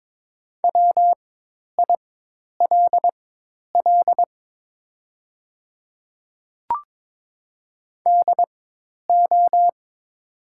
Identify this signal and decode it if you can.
Morse code, heard as a keyed tone, about 23 wpm, spaced as for 10 wpm, 700 Hz. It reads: WILL DO
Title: 4x Character Spacing - Top 100 Words - 22wpm